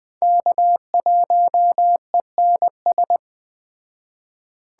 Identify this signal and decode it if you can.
Morse code, heard as a keyed tone, about 20 wpm, 700 Hz. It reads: K1ENS
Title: Speed Racing - USA 1x3 Call Signs 17wpm